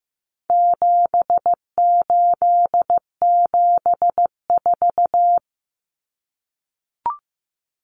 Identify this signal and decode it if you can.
Morse code, heard as a keyed tone, about 15 wpm, 700 Hz. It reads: 7874
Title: Common Cut Numbers - 4-Digit 15wpm